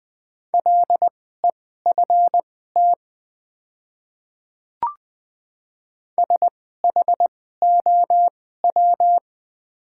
Morse code, heard as a keyed tone, about 20 wpm, spaced as for 14 wpm, 700 Hz. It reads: LEFT SHOW
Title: 2x Character Spacing - Top 300 Words - 20wpm